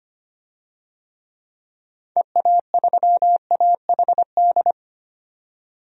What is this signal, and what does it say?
Morse code, heard as a keyed tone, about 25 wpm, 690 Hz. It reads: EA3AHD
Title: DXCC-100 Call Signs - 6 Characters - 25wpm